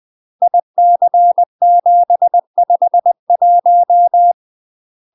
Morse code, heard as a keyed tone, about 20 wpm, 700 Hz. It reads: IC751
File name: UK MASTER QSOs TRAINING / 20WPM / UK MASTER QSOs 0066